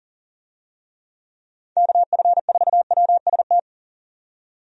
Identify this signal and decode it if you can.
Morse code, heard as a keyed tone, about 40 wpm, 700 Hz. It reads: KF4WST
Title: USA Call Signs - 6 Characters - 40wpm